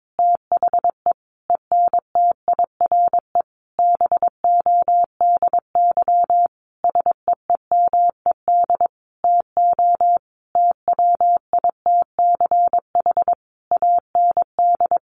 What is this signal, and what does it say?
Morse code, heard as a keyed tone, about 22 wpm, 700 Hz. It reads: THE ENTIRE BODY SEEMED TO TWITCH AND
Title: A Sound Of Thunder 22wpm